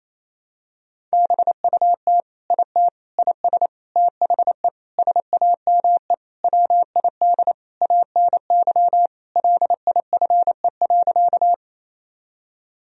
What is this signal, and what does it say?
Morse code, heard as a keyed tone, about 28 wpm, 700 Hz. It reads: BUT IT IS THE SAME WID ANY LIFE.
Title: Sentences from Top 300 Words and Abbreviations 28wpm